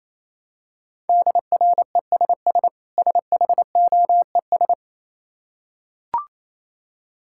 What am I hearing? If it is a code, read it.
Morse code, heard as a keyed tone, about 27 wpm, 690 Hz. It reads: DRESS SHOES
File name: Easy Words - Speed-Racing with Hints - 22wpm